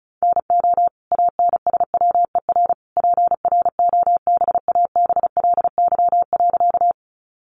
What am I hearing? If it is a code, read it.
Morse code, heard as a keyed tone, about 34 wpm, 700 Hz. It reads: NO ANSWER PROBABLY.
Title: Sentences from Top 700 Words 35wpm